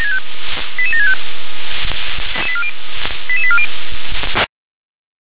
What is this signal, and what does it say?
Five-tone selective call (ZVEI), heard as quick runs of five tones, tones of about 70 ms. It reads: 88765 88840
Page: ZVEI